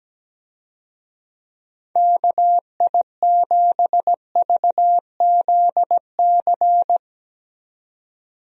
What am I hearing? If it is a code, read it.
Morse code, heard as a keyed tone, about 17 wpm, 700 Hz. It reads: KI7VZC
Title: USA Call Signs - 6 Characters - 17wpm